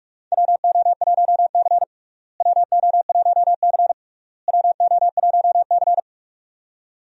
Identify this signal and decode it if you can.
Morse code, heard as a keyed tone, about 43 wpm, 700 Hz. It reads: WO1C WO1C WO1C
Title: USA Call Signs - 4 Characters - Repeated 3x - 45wpm